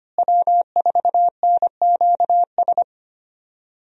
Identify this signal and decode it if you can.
Morse code, heard as a keyed tone, about 25 wpm, 700 Hz. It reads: W4NQS
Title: USA Call Signs - 5 Characters - Repeated 2x - 25wpm